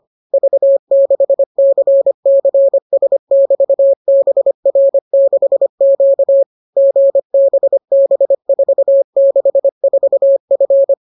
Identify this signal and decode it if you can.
Morse code, heard as a keyed tone, about 25 wpm, 550 Hz. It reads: V6CCS<BT>BR6Q GBB464F